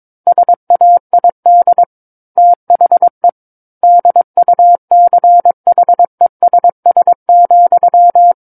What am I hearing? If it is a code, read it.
Morse code, heard as a keyed tone, about 22 wpm, 700 Hz. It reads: SAID THE DUCHESS,
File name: 06-Chapter-06 22 WPM